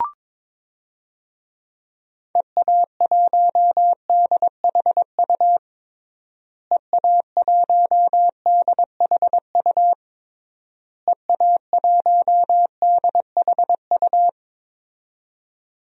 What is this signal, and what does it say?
Morse code, heard as a keyed tone, about 22 wpm, 700 Hz. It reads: EA1DHU EA1DHU EA1DHU
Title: DXCC-100 Call Signs - 6 Characters - Repeated 3x - 22wpm